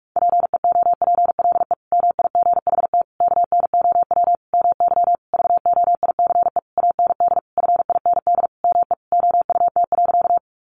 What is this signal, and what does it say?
Morse code, heard as a keyed tone, about 44 wpm, 700 Hz. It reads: PEOPLE MIGHT KNOW MY VOICE AND FIND ME OUT.
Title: Sentences from Top 500 Words 45wpm